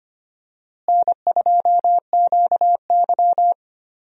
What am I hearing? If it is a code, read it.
Morse code, heard as a keyed tone, about 25 wpm, 700 Hz. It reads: N2QY
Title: USA Call Signs - 4 Characters - Repeated 2x - 25wpm